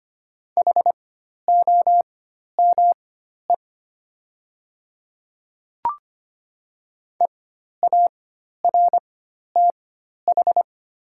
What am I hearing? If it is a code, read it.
Morse code, heard as a keyed tone, about 25 wpm, spaced as for 12 wpm, 700 Hz. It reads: HOME EARTH
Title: 4x Character Spacing - Top 200 Words - 25wpm